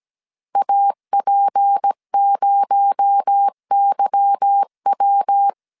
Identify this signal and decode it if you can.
Morse code, heard as a keyed tone, about 17 wpm, 800 Hz. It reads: AP0YW
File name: CW / DET / 1 9 1500C S A 16wpm 2828